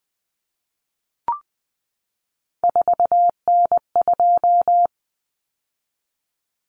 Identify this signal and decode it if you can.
Morse code, heard as a keyed tone, about 20 wpm, 700 Hz. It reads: 4N2
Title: Common Cut Numbers - 3-Digit 20wpm